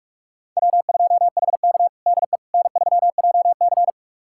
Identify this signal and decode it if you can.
Morse code, heard as a keyed tone, about 46 wpm, 700 Hz. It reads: W2HK DE N3JC
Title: USA Callsign DE Callsign - 4 Characters - 45wpm